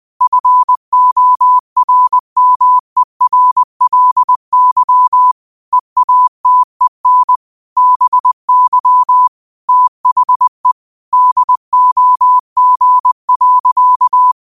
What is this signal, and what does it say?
Morse code, heard as a keyed tone, about 20 wpm, 1000 Hz. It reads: FORMERLY EATEN BY THE DOG.